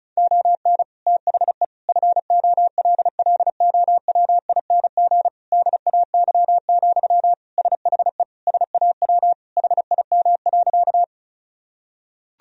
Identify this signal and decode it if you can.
Morse code, heard as a keyed tone, about 36 wpm, 700 Hz. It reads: ON THE FOLLOWING DAY, SHE SAW HIM.
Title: Sentences from Top 1000 Words - Review Part 4 - 35wpm